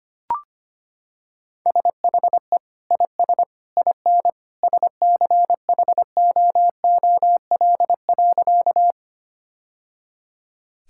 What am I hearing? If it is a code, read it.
Morse code, heard as a keyed tone, about 25 wpm, 700 Hz. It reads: SHE IS IN SCHOOL.
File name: practice / 25wpm / Sentences with 2-Word Phrases - 25wpm